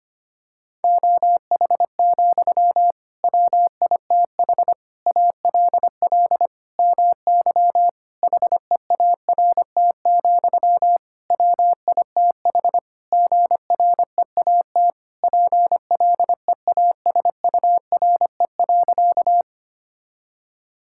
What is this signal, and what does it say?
Morse code, heard as a keyed tone, about 25 wpm, 700 Hz. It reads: OH, WITH ALL MY HEART, WITH GREAT PLEASURE.
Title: Sentences from Top 1600 Words 25wpm